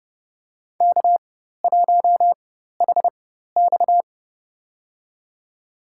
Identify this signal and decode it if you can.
Morse code, heard as a keyed tone, about 32 wpm, spaced as for 14 wpm, 700 Hz. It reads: K1HX
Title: USA Call Signs - 4 Characters - 4x Character Spacing - 30wpm